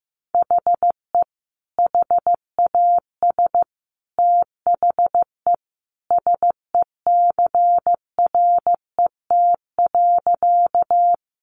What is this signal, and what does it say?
Morse code, heard as a keyed tone, about 15 wpm, 700 Hz. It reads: HE HAS THE SECRET.
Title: Sentences from Top 1200 Words 15wpm